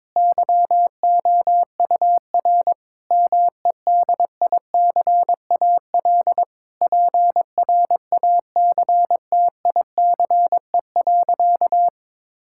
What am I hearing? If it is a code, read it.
Morse code, heard as a keyed tone, about 22 wpm, 700 Hz. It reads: YOUR MEDICAL PRACTICE.
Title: Sentences from Top 1000 Words - Review Part 5 - 22wpm